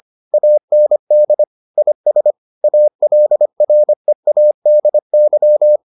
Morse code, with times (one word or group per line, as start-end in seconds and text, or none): 0.3-1.4 AND
1.8-2.3 IS
2.6-5.8 ALREADY